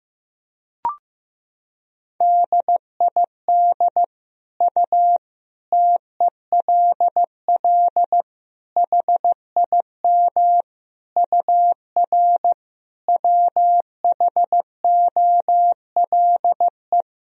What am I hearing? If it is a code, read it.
Morse code, heard as a keyed tone, about 16 wpm, 700 Hz. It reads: DID U TELL HIM UR WHOLE
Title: Sentences from Top 400 Words and Abbreviations 15wpm